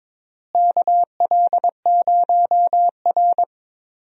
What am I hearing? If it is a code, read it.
Morse code, heard as a keyed tone, about 22 wpm, 700 Hz. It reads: KL0R
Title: USA Call Signs - 4 Characters - Repeated 2x - 22wpm